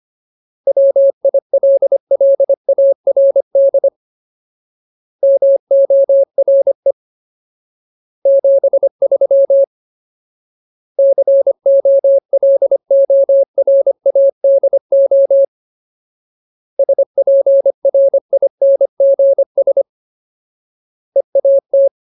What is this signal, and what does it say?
Morse code, heard as a keyed tone, about 25 wpm, 550 Hz. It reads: WILLARD MORE 73 COLORADO SPRINGS EAT
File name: Ultimate Word List / 25WPM Ultimate